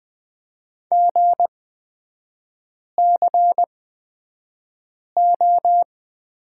Morse code, heard as a keyed tone, about 20 wpm, spaced as for 5 wpm, 700 Hz. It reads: GCO
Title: 236 Lesson 38e - Letter q - Three Characters 20wpm